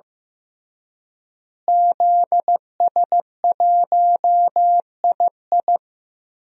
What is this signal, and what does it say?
Morse code, heard as a keyed tone, about 16 wpm, 700 Hz. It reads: ZS1II
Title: DXCC-100 Call Signs - 5 Characters - Repeated 3x - 15wpm